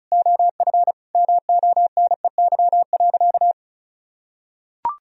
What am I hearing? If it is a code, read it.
Morse code, heard as a keyed tone, about 34 wpm, 700 Hz. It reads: OF MONEY.
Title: Sentences from Top 1000 Words - Review Part 4 - 35wpm